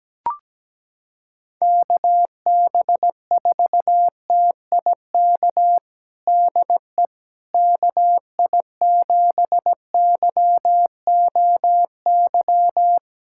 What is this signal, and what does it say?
Morse code, heard as a keyed tone, about 17 wpm, 700 Hz. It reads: KB4TIK DE KI7YOY
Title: USA Callsign DE Callsign - 6 Characters - 17wpm